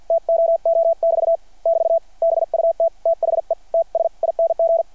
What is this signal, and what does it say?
Morse code, heard as a keyed tone, about 49 wpm, 660 Hz. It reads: TOO<BT> <BT> BUT THE TS ING
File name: archived / CRSnet